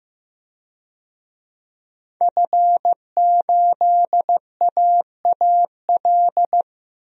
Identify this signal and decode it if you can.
Morse code, heard as a keyed tone, about 15 wpm, 700 Hz. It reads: F8AAL
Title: DXCC-100 Call Signs - 5 Characters - 15wpm